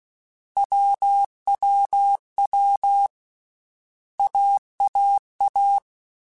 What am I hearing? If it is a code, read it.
Morse code, heard as a keyed tone, about 16 wpm, 800 Hz. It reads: WWW AAA
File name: Basic Sets / Basic Letter Groups Collective7 / AWUNDG@13